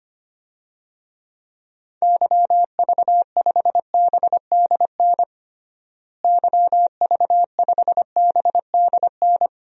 Morse code, heard as a keyed tone, about 25 wpm, 700 Hz. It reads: YV5BDN YV5BDN
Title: DXCC-100 Call Signs - 6 Characters - Repeated 2x - 25wpm